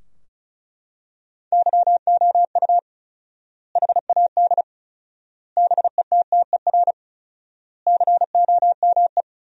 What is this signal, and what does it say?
Morse code, heard as a keyed tone, about 34 wpm, 700 Hz. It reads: YOU HAD BETTER COME
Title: 4x Word-Spacing - Sentences from Top 400 Words - 35wpm